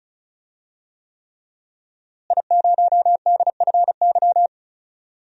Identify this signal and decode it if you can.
Morse code, heard as a keyed tone, about 34 wpm, 700 Hz. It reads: I0DFY
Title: DXCC-100 Call Signs - 5 Characters - Repeated 3x - 35wpm